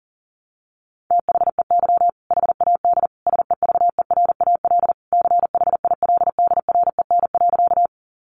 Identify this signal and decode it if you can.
Morse code, heard as a keyed tone, about 40 wpm, 700 Hz. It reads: THEY HAD SEVERAL CHILDREN.
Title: Sentences from Top 400 Words 40wpm